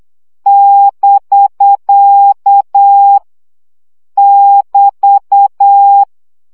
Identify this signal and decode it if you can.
Morse code, heard as a keyed tone, about 8 wpm, 800 Hz. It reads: <BK> <BT>
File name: CW gyakorlas bk 7 baud